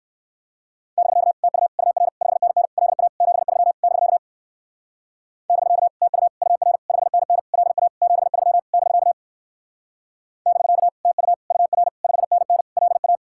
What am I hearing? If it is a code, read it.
Morse code, heard as a keyed tone, about 56 wpm, 700 Hz. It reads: <BK> TU UR 5NN LA 73 <BK> <BK> TU UR 5NN LA 73 <BK> <BK> TU UR 5NN LA
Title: Speed-Racing Common QSO Phrases - 45wpm